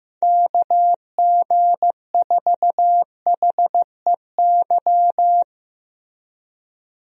Morse code, heard as a keyed tone, about 15 wpm, 700 Hz. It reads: KG4HEY
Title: USA Call Signs - 6 Characters - Repeated 3x - 15wpm